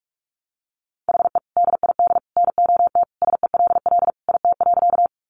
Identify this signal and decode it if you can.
Morse code, heard as a keyed tone, about 44 wpm, 700 Hz. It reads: HE DID NOT SELL IT.
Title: Sentences from Top 1100 Words 45wpm